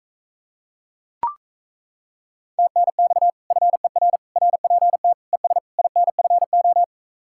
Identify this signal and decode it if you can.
Morse code, heard as a keyed tone, about 35 wpm, 700 Hz. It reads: TNX FER RPT ES INFO
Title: Speed-Racing Common QSO Phrases - 28wpm